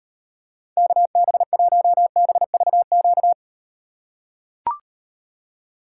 Morse code, heard as a keyed tone, about 38 wpm, 700 Hz. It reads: KB1BVQ